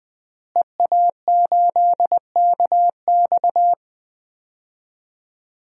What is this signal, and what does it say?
Morse code, heard as a keyed tone, about 20 wpm, 700 Hz. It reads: EA8KX
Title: DXCC-100 Call Signs - 5 Characters - Repeated 3x - 20wpm